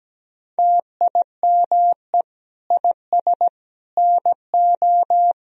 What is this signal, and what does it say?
Morse code, heard as a keyed tone, about 17 wpm, 700 Hz. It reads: TIME IS NO
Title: Sentences from Top 1000 Words - Review Part 3 - 17wpm